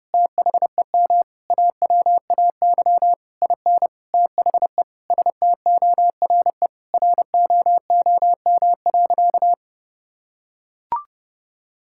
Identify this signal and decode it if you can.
Morse code, heard as a keyed tone, about 30 wpm, 700 Hz. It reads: THEM AWAY IN THE STORE ROOM.
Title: Sentences from Top 1000 Words - Review Part 5 - 30wpm